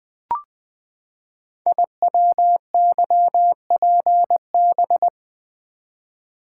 Morse code, heard as a keyed tone, about 20 wpm, 700 Hz. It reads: IWYPB
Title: Code Groups - 5 Letters - 20wpm